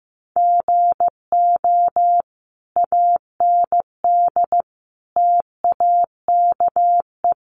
Morse code, heard as a keyed tone, about 15 wpm, 700 Hz. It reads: GO AND TAKE
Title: Sentences from Top 1400 Words 15wpm